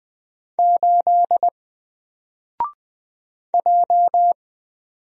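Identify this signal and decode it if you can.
Morse code, heard as a keyed tone, about 20 wpm, 700 Hz. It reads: 8 J
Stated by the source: Single letter and number Morse drill at 20 wpm, from N4LCD.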